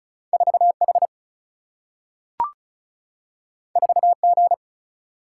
Morse code, heard as a keyed tone, about 36 wpm, 700 Hz. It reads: 4 H 4 G
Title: Single-Letter Single-Number 35wpm